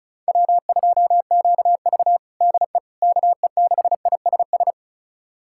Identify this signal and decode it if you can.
Morse code, heard as a keyed tone, about 34 wpm, 700 Hz. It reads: W2QV DE KE6ISS
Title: USA Callsign DE Callsign 35wpm